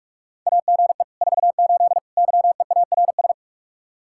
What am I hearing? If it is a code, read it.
Morse code, heard as a keyed tone, about 61 wpm, spaced as for 53 wpm, 700 Hz. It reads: AGE 48 YEARS
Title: Speed-Racing Common QSO Phrases Encore - 45wpm